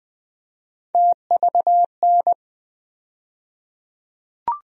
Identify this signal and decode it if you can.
Morse code, heard as a keyed tone, about 20 wpm, 700 Hz. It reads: TVN
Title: USA Call Sign Postfixes - 3 Letters 20wpm